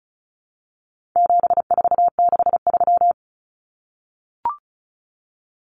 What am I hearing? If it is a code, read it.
Morse code, heard as a keyed tone, about 34 wpm, 700 Hz. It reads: 7463
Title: Common Cut Numbers - 4-Digit 35wpm